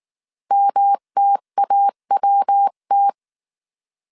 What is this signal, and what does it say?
Morse code, heard as a keyed tone, about 19 wpm, 800 Hz. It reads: MTAWT